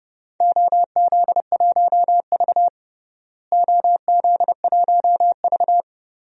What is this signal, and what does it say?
Morse code, heard as a keyed tone, about 32 wpm, 700 Hz. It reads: OZ1V OZ1V
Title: DXCC-100 Call Signs - 4 Characters - Repeated 2x - 30wpm